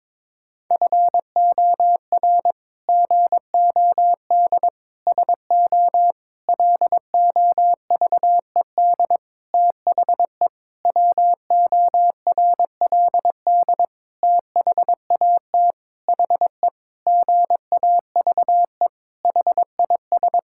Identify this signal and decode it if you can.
Morse code, heard as a keyed tone, about 22 wpm, 700 Hz. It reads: FOR GOD SO LOVED THE WORLD THAT HE GAVE HIS
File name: practice / 22wpm / Sentences with 8-Word Phrases - 22wpm